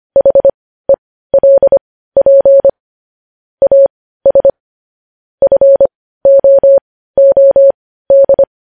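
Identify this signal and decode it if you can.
Morse code, heard as a keyed tone, about 25 wpm, spaced as for 15 wpm, 550 Hz. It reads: HELP AS FOOD
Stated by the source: Morse news headlines practice at 15 wpm.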